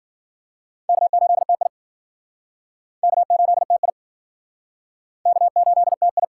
Morse code, heard as a keyed tone, about 53 wpm, 700 Hz. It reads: K8TI K8TI K8TI